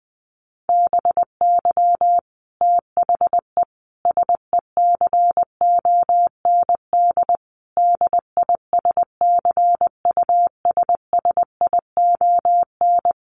Morse code, heard as a keyed tone, about 20 wpm, 700 Hz. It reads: BY THE SECOND DISCUSSION
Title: Sentences from Top 1200 Words 20wpm